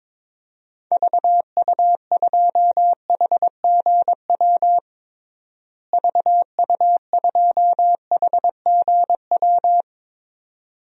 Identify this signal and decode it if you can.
Morse code, heard as a keyed tone, about 22 wpm, 700 Hz. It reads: VU2HGW VU2HGW
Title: DXCC-100 Call Signs - 6 Characters - Repeated 2x - 22wpm